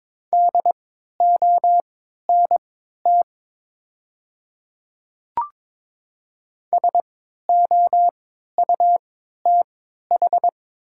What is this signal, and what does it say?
Morse code, heard as a keyed tone, about 22 wpm, spaced as for 12 wpm, 690 Hz. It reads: DONT SOUTH
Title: 3x Character Spacing - Top 500 Words - 22wpm